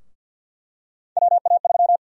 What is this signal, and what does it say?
Morse code, heard as a keyed tone, about 52 wpm, 700 Hz. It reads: WA3
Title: USA Call Sign Prefixes - Repeated 2x - 50wpm